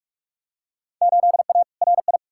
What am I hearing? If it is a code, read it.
Morse code, heard as a keyed tone, about 46 wpm, 700 Hz. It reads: 8A RI
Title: Field Day Exchanges Encore - 45wpm